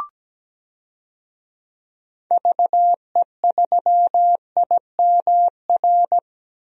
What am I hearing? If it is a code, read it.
Morse code, heard as a keyed tone, about 17 wpm, 700 Hz. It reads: VE3IMR
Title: DXCC-100 Call Signs - 6 Characters - 17wpm